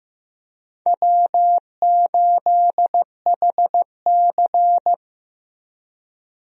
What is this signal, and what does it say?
Morse code, heard as a keyed tone, about 15 wpm, 700 Hz. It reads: W8HC
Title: USA Call Signs - 4 Characters - 15wpm